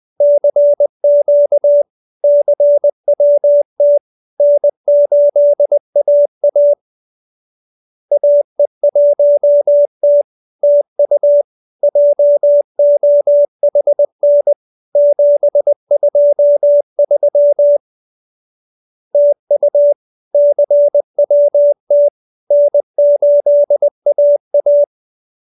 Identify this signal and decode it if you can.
Morse code, heard as a keyed tone, about 20 wpm, 580 Hz. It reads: CQ CWT N8AA AE1T TU JOHN 723 TU CWT N8AA